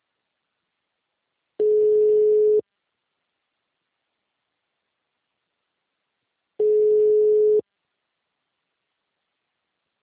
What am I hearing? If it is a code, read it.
Ringback tone (Europe, 425 Hz).